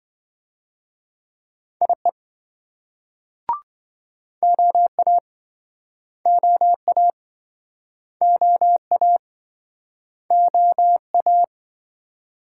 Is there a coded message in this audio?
Yes: Morse code from 1.8 to 11.4 s, heard as a keyed tone, about 26 wpm, 700 Hz. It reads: IE OA OA OA OA